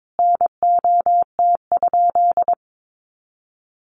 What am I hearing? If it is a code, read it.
Morse code, heard as a keyed tone, about 22 wpm, 700 Hz. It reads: NOT?
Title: Sentences from Top 600 Words 22wpm